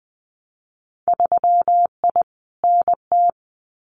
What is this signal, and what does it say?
Morse code, heard as a keyed tone, about 20 wpm, 700 Hz. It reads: 3I NT
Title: Winter Field Day Exchanges Encore 20wpm